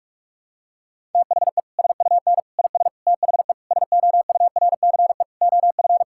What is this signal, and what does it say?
Morse code, heard as a keyed tone, about 47 wpm, 700 Hz. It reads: THE SUN IS THE SOURCE OF
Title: Sentences from Top 1000 Words - Review Part 5 - 45wpm